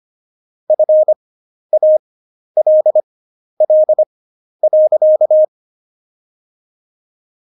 Morse code, heard as a keyed tone, about 25 wpm, spaced as for 11 wpm, 620 Hz. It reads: FALL.
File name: Jokes-25-char-10-eff